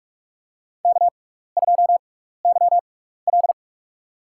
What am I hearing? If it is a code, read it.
Morse code, heard as a keyed tone, about 47 wpm, spaced as for 16 wpm, 700 Hz. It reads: K2YL
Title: USA Call Signs - 4 Characters - 6x Character Spacing - 45wpm